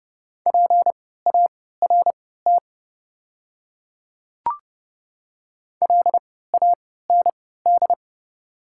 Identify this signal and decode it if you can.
Morse code, heard as a keyed tone, about 30 wpm, spaced as for 17 wpm, 700 Hz. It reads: PART LAND
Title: 3x Character Spacing - Top 200 Words - 30wpm